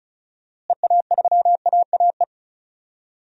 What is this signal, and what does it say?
Morse code, heard as a keyed tone, about 36 wpm, 700 Hz. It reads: EA3AAE
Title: DXCC-100 Call Signs - 6 Characters - Repeated 3x - 35wpm